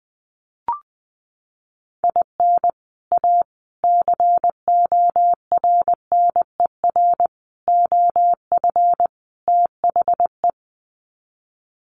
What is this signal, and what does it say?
Morse code, heard as a keyed tone, about 20 wpm, 700 Hz. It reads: IN A CORNER OF THE
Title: Sets of 5 Words Encore2 20wpm